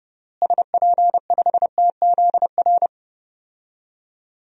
Morse code, heard as a keyed tone, about 30 wpm, 700 Hz. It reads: SP5TZR
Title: DXCC-100 Call Signs - 6 Characters - Repeated 3x - 30wpm